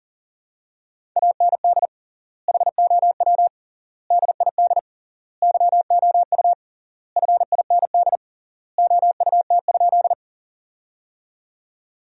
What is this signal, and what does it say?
Morse code, heard as a keyed tone, about 40 wpm, 700 Hz. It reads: AND HOW DID YOU FIND OUT?
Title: 3x Word-Spacing - Sentences from Top 100 Words - 40wpm